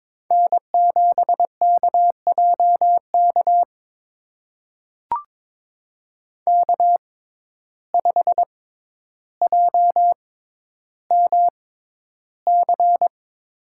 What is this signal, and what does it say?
Morse code, heard as a keyed tone, about 22 wpm, spaced as for 11 wpm, 700 Hz. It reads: N7KJK K5JMC
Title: USA Call Signs - 5 Characters - 6x Character Spacing - 22wpm